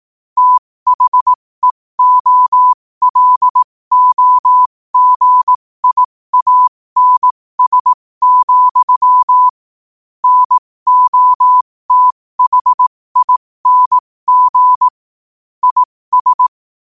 Morse code, heard as a keyed tone, about 18 wpm, spaced as for 15 wpm, 1000 Hz. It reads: THEOLOGIANS, NOTHING IS
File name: qotdfiles / 15wpm